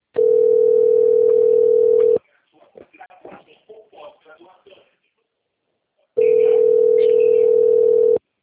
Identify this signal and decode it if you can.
Ringback tone (North America), 440 + 480 Hz, with speech.
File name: rg-1600-unknown- - - .42921